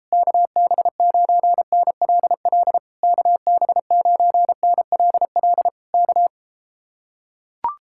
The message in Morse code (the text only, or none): KB9NLL KB9NLL K